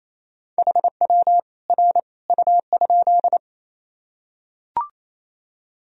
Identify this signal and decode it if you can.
Morse code, heard as a keyed tone, about 28 wpm, 700 Hz. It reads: HW R U?
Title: Sentences from Top 100 Words and Abbreviations 28wpm